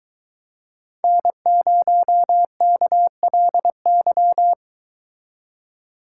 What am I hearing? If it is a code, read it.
Morse code, heard as a keyed tone, about 22 wpm, 700 Hz. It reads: N0KLY